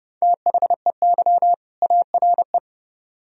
Morse code, heard as a keyed tone, about 40 wpm, 700 Hz. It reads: THEY ARE